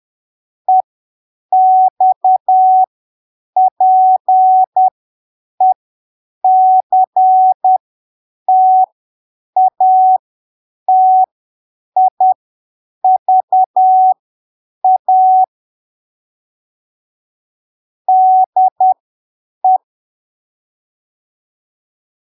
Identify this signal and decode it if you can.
Morse code, heard as a keyed tone, about 10 wpm, spaced as for 7 wpm, 750 Hz. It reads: EXPECTATIVA DE
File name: arquivos / texto / texto 19 10ppm